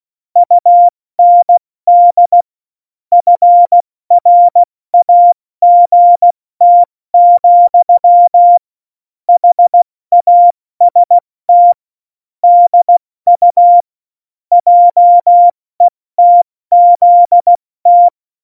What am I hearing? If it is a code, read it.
Morse code, heard as a keyed tone, about 16 wpm, spaced as for 14 wpm, 700 Hz. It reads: UND FRAGT, HAST DU JETZT
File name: Morsekurs / Witze / Witz 14 April 2020, 70 F80